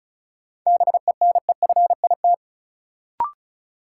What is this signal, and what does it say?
Morse code, heard as a keyed tone, about 36 wpm, 700 Hz. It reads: BENEFIT
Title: Sentences from Top 1000 Words - Review Part 6 - New Word Drill - 35wpm